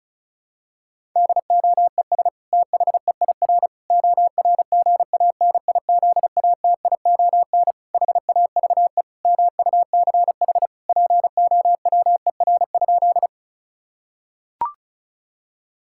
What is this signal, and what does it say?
Morse code, heard as a keyed tone, about 34 wpm, 700 Hz. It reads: DOES THEIR ORGANIZATION HAVE MUCH POWER?
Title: Sentences from Top 1000 Words - Review Part 5 - 35wpm